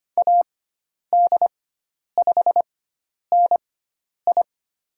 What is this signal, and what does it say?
Morse code, heard as a keyed tone, about 25 wpm, spaced as for 10 wpm, 700 Hz. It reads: AD5NI
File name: Lesson 16g - Letter c - Call Signs 25wpm